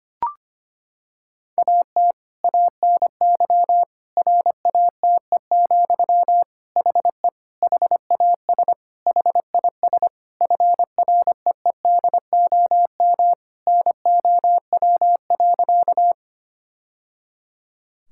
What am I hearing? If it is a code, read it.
Morse code, heard as a keyed tone, about 25 wpm, 700 Hz. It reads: AT ANY RATE, HE HAS HIS FREEDOM NOW.
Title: Sentences from Top 1000 Words - Review Part 5 - 25wpm